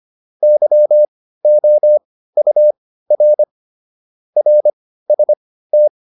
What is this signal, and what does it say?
Morse code, heard as a keyed tone, about 25 wpm, spaced as for 15 wpm, 600 Hz. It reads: YOUR RST
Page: intermediate-practice-files